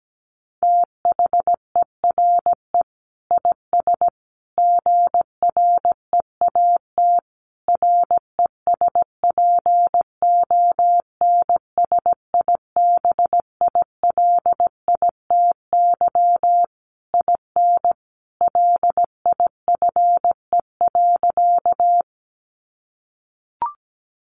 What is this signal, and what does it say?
Morse code, heard as a keyed tone, about 17 wpm, 700 Hz. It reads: THERE IS GREAT RESPONSIBILITY IN LIFE.
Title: Sentences from Top 1200 Words 17wpm